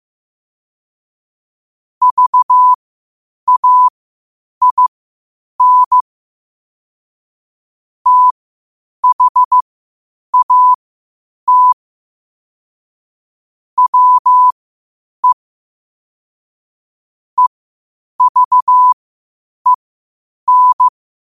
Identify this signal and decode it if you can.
Morse code, heard as a keyed tone, about 15 wpm, spaced as for 8 wpm, 1000 Hz. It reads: VAIN THAT WE EVEN
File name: qotdfiles / 8wpm